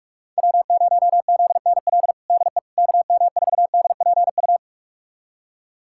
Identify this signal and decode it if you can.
Morse code, heard as a keyed tone, about 44 wpm, 700 Hz. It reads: W0ZNL DE KM4DPU
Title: USA Callsign DE Callsign 45wpm